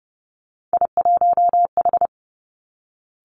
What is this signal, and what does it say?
Morse code, heard as a keyed tone, about 30 wpm, 700 Hz. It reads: I1H
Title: Three Challenging Letter-Number Combinations 30wpm